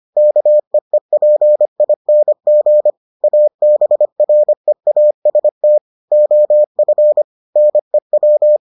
Morse code, heard as a keyed tone, about 25 wpm, 600 Hz. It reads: KEEPING ABREAST OF NEW